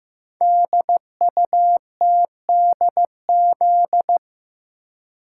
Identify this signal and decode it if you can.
Morse code, heard as a keyed tone, about 15 wpm, 700 Hz. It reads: DUTDZ
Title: Code Groups - 5 Letters - 15wpm